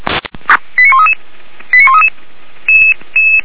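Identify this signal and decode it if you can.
Five-tone selective call (ZVEI), heard as quick runs of five tones, tones of about 70 ms. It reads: 89133 89133
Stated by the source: ZVEI five-tone selective call received by SDR.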